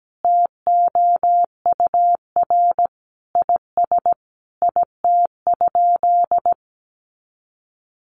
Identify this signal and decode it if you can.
Morse code, heard as a keyed tone, about 17 wpm, 700 Hz. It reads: TOUR IS IT?
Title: Sentences from Top 1400 Words 17wpm